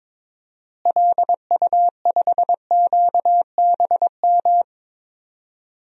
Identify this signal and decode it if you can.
Morse code, heard as a keyed tone, about 22 wpm, 700 Hz. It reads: LU5QBM
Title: DXCC-100 Call Signs - 6 Characters - Repeated 2x - 22wpm